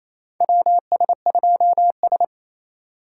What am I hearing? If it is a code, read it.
Morse code, heard as a keyed tone, about 27 wpm, 700 Hz. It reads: WS2S
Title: USA Call Signs - 4 Characters - 4x Character Spacing - 28wpm